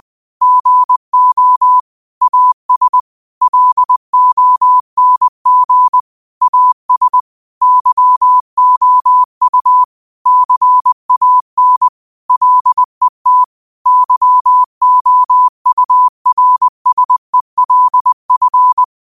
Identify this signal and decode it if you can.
Morse code, heard as a keyed tone, about 20 wpm, 1000 Hz. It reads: GO AS LONG AS YOU CAN LET YOURSELF
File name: qotdfiles / 20wpm